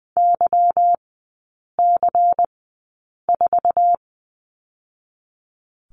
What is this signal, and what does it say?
Morse code, heard as a keyed tone, about 20 wpm, 700 Hz. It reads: Y C 4
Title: 145 Lesson 24e - Letter y - Three Characters 20wpm